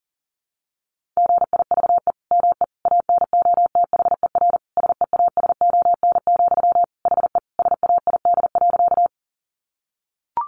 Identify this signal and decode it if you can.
Morse code, heard as a keyed tone, about 40 wpm, 700 Hz. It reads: GIVE ME ANOTHER SEASON, HE SAID.
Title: Sentences from Top 1400 Words 40wpm